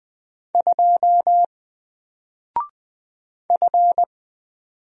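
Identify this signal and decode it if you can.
Morse code, heard as a keyed tone, about 20 wpm, 700 Hz. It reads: 2 F
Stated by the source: Single letter and number Morse drill at 20 wpm, from N4LCD.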